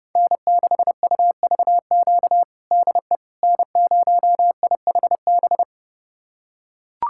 Morse code, heard as a keyed tone, about 30 wpm, 700 Hz. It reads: N6UVQ DE N0IHB
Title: USA Callsign DE Callsign - 5 Characters - 30wpm